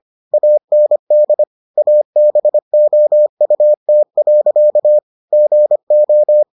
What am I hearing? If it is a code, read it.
Morse code, heard as a keyed tone, about 25 wpm, 600 Hz. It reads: AND ABOUT. GO